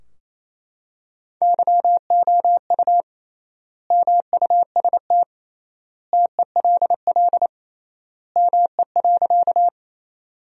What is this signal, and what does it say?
Morse code, heard as a keyed tone, about 28 wpm, 700 Hz. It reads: YOU MUST TELL ME.
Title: 3x Word-Spacing - Sentences from Top 200 Words - 28wpm